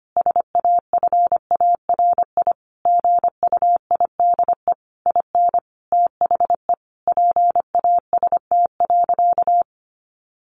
Morse code, heard as a keyed tone, about 25 wpm, 700 Hz. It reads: SAFARI GUIDE IN THE PAST.
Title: A Sound Of Thunder 25wpm